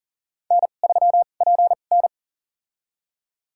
Morse code, heard as a keyed tone, about 40 wpm, spaced as for 29 wpm, 700 Hz. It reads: N3PN
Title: USA Call Signs - 4 Characters - 2x Character Spacing - 40wpm